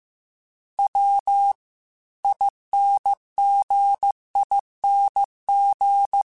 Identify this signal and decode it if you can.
Morse code, heard as a keyed tone, about 15 wpm, 800 Hz. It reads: W INGING